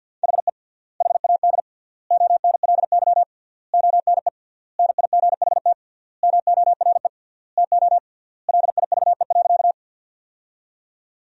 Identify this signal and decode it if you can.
Morse code, heard as a keyed tone, about 50 wpm, 700 Hz. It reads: HE HAD ONLY ONE NIGHT MORE TO LIVE.
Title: 3x Word-Spacing - Sentences from Top 300 Words - 50wpm